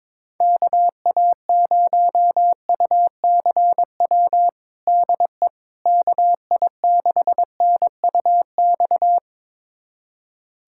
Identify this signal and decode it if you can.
Morse code, heard as a keyed tone, about 22 wpm, 700 Hz. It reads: KA0UCW DE KI6NUX
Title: USA Callsign DE Callsign - 6 Characters - 22wpm